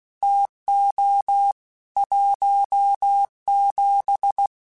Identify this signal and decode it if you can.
Morse code, heard as a keyed tone, about 16 wpm, 800 Hz. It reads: TO 17
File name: Weather Sets / Weather Sets-mp3s / Weather 4 21 @15